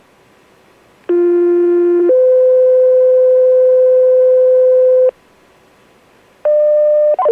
Two-tone page, heard as a one-second tone, then a three-second tone, about 350 then 510 Hz. It is heard from 1.1 to 5.1 s.